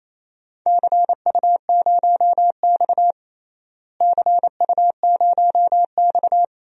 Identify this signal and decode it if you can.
Morse code, heard as a keyed tone, about 28 wpm, 700 Hz. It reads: CU0X CU0X
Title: DXCC-100 Call Signs - 4 Characters - Repeated 2x - 28wpm